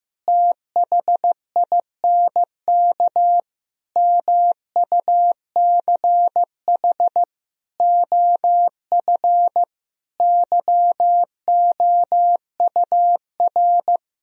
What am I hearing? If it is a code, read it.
Morse code, heard as a keyed tone, about 15 wpm, 700 Hz. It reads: THINK MUCH OF YOUR
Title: Sentences from Top 1000 Words - Review Part 2 - 15wpm